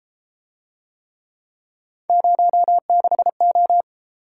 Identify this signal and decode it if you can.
Morse code, heard as a keyed tone, about 33 wpm, 700 Hz. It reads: 06O